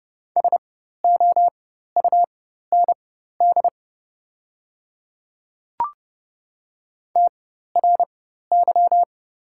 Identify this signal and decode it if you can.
Morse code, heard as a keyed tone, about 30 wpm, spaced as for 14 wpm, 700 Hz. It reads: SOUND TRY
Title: 4x Character Spacing - Top 200 Words - 30wpm